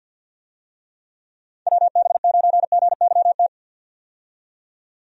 Morse code, heard as a keyed tone, about 52 wpm, 700 Hz. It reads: WB9GYT
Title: USA Call Signs - 6 Characters - 50wpm